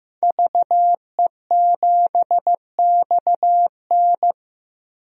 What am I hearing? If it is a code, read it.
Morse code, heard as a keyed tone, about 16 wpm, 700 Hz. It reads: VE7XN
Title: DXCC-100 Call Signs - 5 Characters - 15wpm